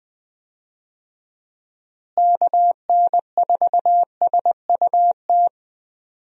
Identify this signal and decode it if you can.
Morse code, heard as a keyed tone, about 20 wpm, 700 Hz. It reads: KN4SUT
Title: USA Call Signs - 6 Characters - Repeated 2x - 20wpm